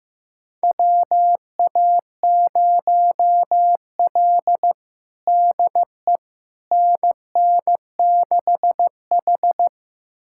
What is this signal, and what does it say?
Morse code, heard as a keyed tone, about 16 wpm, 700 Hz. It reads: WA0L DE NN6H
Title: USA Callsign DE Callsign - 4 Characters - 15wpm